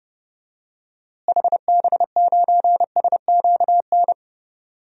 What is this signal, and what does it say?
Morse code, heard as a keyed tone, about 32 wpm, 700 Hz. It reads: HB9SQN
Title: DXCC-100 Call Signs - 6 Characters - Repeated 2x - 30wpm